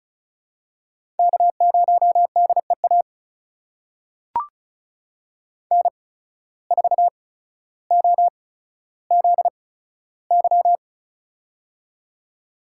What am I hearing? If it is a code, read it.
Morse code, heard as a keyed tone, about 36 wpm, spaced as for 15 wpm, 700 Hz. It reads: K0DEA N4OZY
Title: USA Call Signs - 5 Characters - 8x Character Spacing - 35wpm